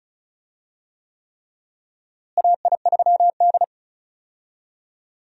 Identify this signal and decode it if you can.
Morse code, heard as a keyed tone, about 34 wpm, 700 Hz. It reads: AI3D
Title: USA Call Signs - 4 Characters - Repeated 3x - 35wpm